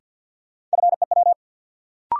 Morse code, heard as a keyed tone, about 53 wpm, 700 Hz. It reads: FEW
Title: Sentences from Top 300 Words - New Word Drill - 50wpm